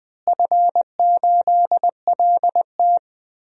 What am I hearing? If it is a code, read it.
Morse code, heard as a keyed tone, about 20 wpm, 700 Hz. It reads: F8LT
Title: DXCC-100 Call Signs - 4 Characters - Repeated 2x - 20wpm